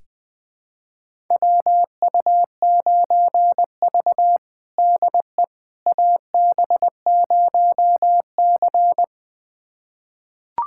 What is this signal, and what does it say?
Morse code, heard as a keyed tone, about 20 wpm, 700 Hz. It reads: WU9V DE AB0C
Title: USA Callsign DE Callsign - 4 Characters - 20wpm